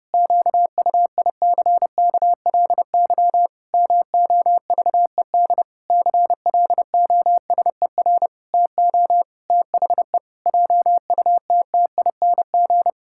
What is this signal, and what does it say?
Morse code, heard as a keyed tone, about 32 wpm, 700 Hz. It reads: QUICKLY MOVED CLOSER TO THE JUTTING
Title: Pangram Challenge v2 - 30wpm